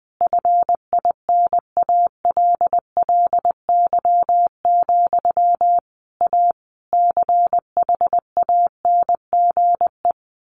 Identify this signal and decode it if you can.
Morse code, heard as a keyed tone, about 20 wpm, 700 Hz. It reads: FINALLY, A CHANGE
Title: A Sound Of Thunder 20wpm